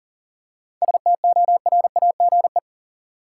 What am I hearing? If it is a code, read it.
Morse code, heard as a keyed tone, about 40 wpm, 700 Hz. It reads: STORAGE
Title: Seven Letter Words - 40wpm